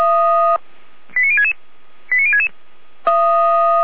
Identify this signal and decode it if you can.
Five-tone selective call (ZVEI), heard as quick runs of five tones, tones of about 70 ms. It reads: 89066 89066